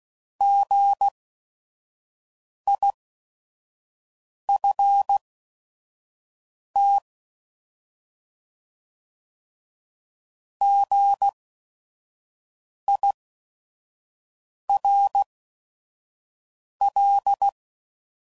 Morse code, heard as a keyed tone, about 16 wpm, spaced as for 5 wpm, 800 Hz. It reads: GIFT GIRL
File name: Prefixes / By First Letter / Gs